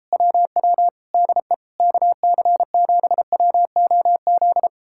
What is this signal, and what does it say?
Morse code, heard as a keyed tone, about 38 wpm, 700 Hz. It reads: WW DE KC7WOZ